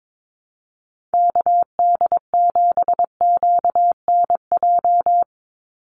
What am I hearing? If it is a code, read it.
Morse code, heard as a keyed tone, about 22 wpm, 700 Hz. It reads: KD7QNJ
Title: USA Call Signs 22wpm